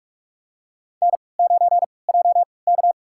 Morse code, heard as a keyed tone, about 44 wpm, spaced as for 25 wpm, 700 Hz. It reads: N9JK
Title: USA Call Signs - 4 Characters - 3x Character Spacing - 45wpm